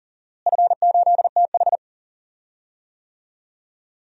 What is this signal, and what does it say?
Morse code, heard as a keyed tone, about 40 wpm, 700 Hz. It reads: F8TH
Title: DXCC-100 Call Signs - 4 Characters - Repeated 2x - 40wpm